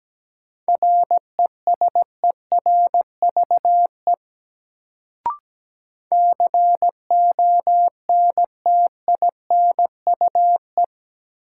Morse code, heard as a keyed tone, about 17 wpm, 700 Hz. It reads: RESERVE CONTINUE
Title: Sentences from Top 1000 Words - Review Part 6 - New Word Drill - 17wpm